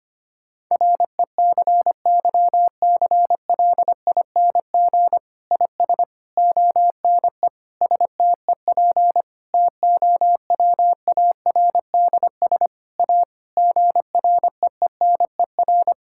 Morse code, heard as a keyed tone, about 25 wpm, 700 Hz. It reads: RECYCLING IS ONE STEP TOWARDS A GREENER
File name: practice / 25wpm / Difficult Sentences with Hints - 25wpm